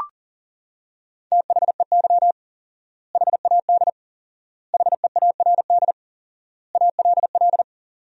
Morse code, heard as a keyed tone, about 37 wpm, 700 Hz. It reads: THEY HAD HEARD ALL
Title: 4x Word-Spacing - Sentences from Top 400 Words - 40wpm